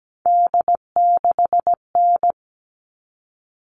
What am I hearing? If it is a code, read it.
Morse code, heard as a keyed tone, about 17 wpm, 700 Hz. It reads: D6N
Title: All Cut Numbers - 3-Digit 17wpm